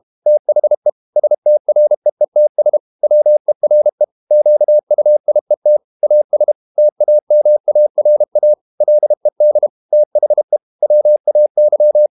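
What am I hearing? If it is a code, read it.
Morse code, heard as a keyed tone, about 32 wpm, 600 Hz. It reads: THE STREETS WERE QUIET AS TAMARA LED THE WAY
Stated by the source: Morse code practice at 32 wpm from CW med Gnister, reading Ken Follett's novel Never in English.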